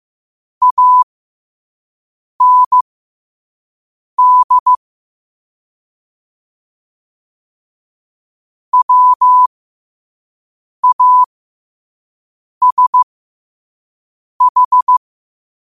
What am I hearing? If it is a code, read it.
Morse code, heard as a keyed tone, about 15 wpm, spaced as for 5 wpm, 1000 Hz. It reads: AND WASH